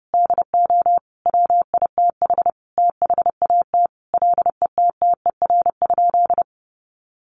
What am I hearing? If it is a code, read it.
Morse code, heard as a keyed tone, about 30 wpm, 700 Hz. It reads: DO WITH THAT LETTER?
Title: Sentences from Top 300 Words 30wpm